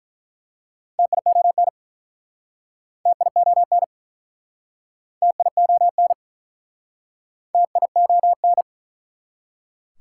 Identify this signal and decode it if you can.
Morse code, heard as a keyed tone, about 49 wpm, 700 Hz. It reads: TION TION TION TION